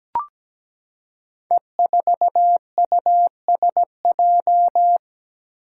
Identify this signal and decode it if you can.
Morse code, heard as a keyed tone, about 17 wpm, 700 Hz. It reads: E4USJ
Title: Five Challenging Letter-Number Combinations 17wpm